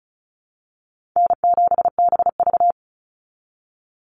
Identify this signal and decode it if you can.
Morse code, heard as a keyed tone, about 34 wpm, 700 Hz. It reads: N7BV
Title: Easy Call Signs Repeated 3x - 35wpm